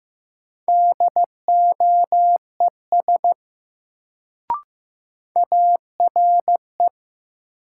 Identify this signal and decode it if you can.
Morse code, heard as a keyed tone, about 15 wpm, 700 Hz. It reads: DOES ARE
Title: Sets of 2 Words - New Word Drill - 15wpm